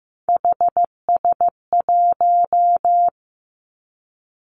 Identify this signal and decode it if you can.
Morse code, heard as a keyed tone, about 15 wpm, 700 Hz. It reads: HS1
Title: Three Challenging Letter-Number Combinations 15wpm